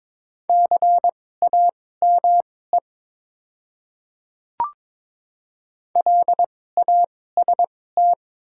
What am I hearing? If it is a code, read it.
Morse code, heard as a keyed tone, about 22 wpm, spaced as for 16 wpm, 700 Hz. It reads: CAME LAST
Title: 2x Character Spacing - Top 300 Words - 22wpm